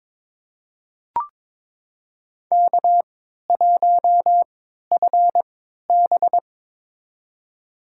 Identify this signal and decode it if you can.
Morse code, heard as a keyed tone, about 22 wpm, spaced as for 12 wpm, 700 Hz. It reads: K1FB
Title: USA Call Signs - 4 Characters - 3x Character Spacing - 22wpm